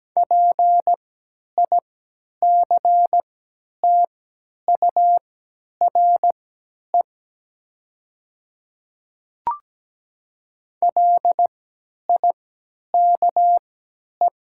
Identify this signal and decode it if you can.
Morse code, heard as a keyed tone, about 17 wpm, spaced as for 10 wpm, 700 Hz. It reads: PICTURE LIKE
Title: 3x Character Spacing - Top 200 Words - 17wpm